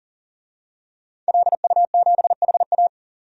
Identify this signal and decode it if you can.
Morse code, heard as a keyed tone, about 40 wpm, 700 Hz. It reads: LU7HA
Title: DXCC-100 Call Signs - 5 Characters - 40wpm